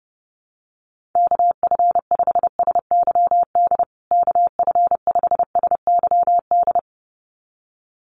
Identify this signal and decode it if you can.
Morse code, heard as a keyed tone, about 30 wpm, 700 Hz. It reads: KF5SYD KF5SYD
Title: USA Call Signs Repeated 2x 30wpm